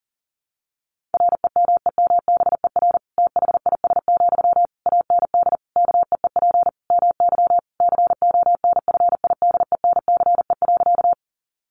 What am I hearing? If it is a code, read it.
Morse code, heard as a keyed tone, about 40 wpm, 700 Hz. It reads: REMEMBER THIS, AND KEEP MY CONFIDENCE.
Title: Sentences from Top 1500 Words 40wpm